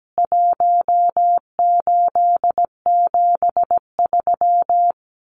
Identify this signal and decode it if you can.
Morse code, heard as a keyed tone, about 17 wpm, 700 Hz. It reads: 1873
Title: CWT Exchange 17wpm